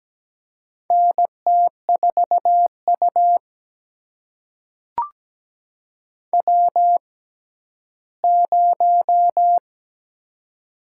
Morse code, heard as a keyed tone, about 17 wpm, spaced as for 11 wpm, 700 Hz. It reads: NT4U W0
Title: USA Call Signs - 6x Character Spacing - 17wpm